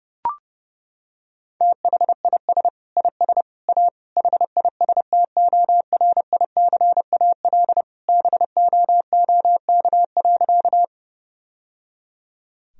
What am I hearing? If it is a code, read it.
Morse code, heard as a keyed tone, about 30 wpm, 700 Hz. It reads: THIS IS A HISTORICAL BOOK.
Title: Sentences from Top 1000 Words - Review Part 4 - 30wpm